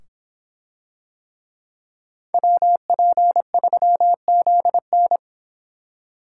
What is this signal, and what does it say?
Morse code, heard as a keyed tone, about 26 wpm, 700 Hz. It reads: WP3ZN